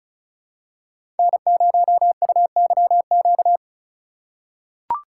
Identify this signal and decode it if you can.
Morse code, heard as a keyed tone, about 36 wpm, 700 Hz. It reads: N0UYQ
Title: USA Call Signs - 2x Character Spacing - 35wpm